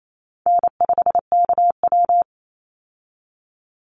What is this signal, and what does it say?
Morse code, heard as a keyed tone, about 28 wpm, 700 Hz. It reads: N5KW
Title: Easy Call Signs - 28wpm